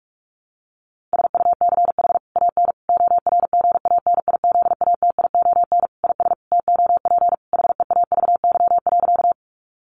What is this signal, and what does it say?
Morse code, heard as a keyed tone, about 44 wpm, 700 Hz. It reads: SUCH AN ORGANIZATION IS TOP HEAVY.
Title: Sentences from Top 1100 Words 45wpm